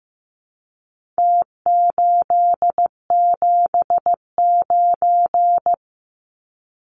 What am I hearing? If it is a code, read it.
Morse code, heard as a keyed tone, about 15 wpm, 700 Hz. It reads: T879
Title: Numbers - Contest Style - 4-Digit 15wpm